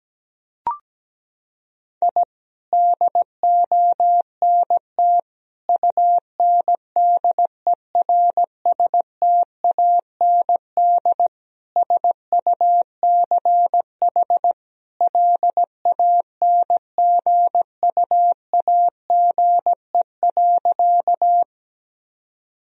Morse code, heard as a keyed tone, about 17 wpm, 700 Hz. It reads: I DONT UNDERSTAND SUCH LANGUAGE.
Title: Sentences from Top 1000 Words - Review Part 3 - 17wpm